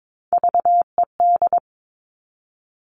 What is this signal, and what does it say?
Morse code, heard as a keyed tone, about 22 wpm, 700 Hz. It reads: VED
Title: All Cut Numbers - 3-Digit 22wpm